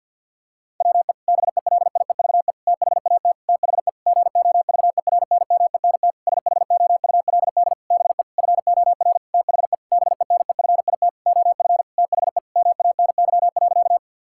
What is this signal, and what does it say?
Morse code, heard as a keyed tone, about 50 wpm, 700 Hz. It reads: WE BELIEVE THAT THE GOVERNMENT SHOULD BE FOR THE BENEFIT OF THE MANY.
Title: Sentences from Top 1000 Words - Review Part 6 - 50wpm